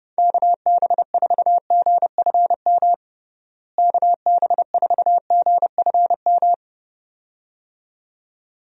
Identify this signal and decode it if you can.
Morse code, heard as a keyed tone, about 30 wpm, 700 Hz. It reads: KB4GFM KB4GFM
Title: USA Call Signs - 6 Characters - Repeated 2x - 30wpm